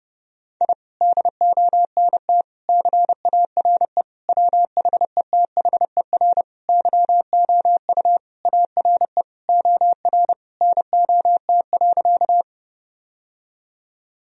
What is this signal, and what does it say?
Morse code, heard as a keyed tone, about 29 wpm, 700 Hz. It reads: I DONT CARE WHETHER YOU ARE OR NOT.
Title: Sentences from Top 1000 Words - Review Part 3 - 30wpm